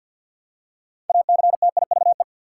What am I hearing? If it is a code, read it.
Morse code, heard as a keyed tone, about 53 wpm, 700 Hz. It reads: ACTIVE
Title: Six Letter Words - 50wpm